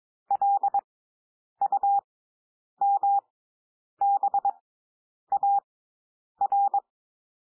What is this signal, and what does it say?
Morse code, heard as a keyed tone, about 22 wpm, spaced as for 9 wpm, 800 Hz. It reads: LUMBAR
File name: SOUND WORDS 22 wpm